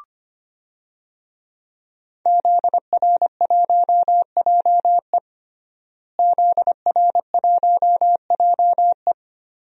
Morse code, heard as a keyed tone, about 25 wpm, 700 Hz. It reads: ZR1JE ZR1JE
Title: DXCC-100 Call Signs - 5 Characters - Repeated 3x - 25wpm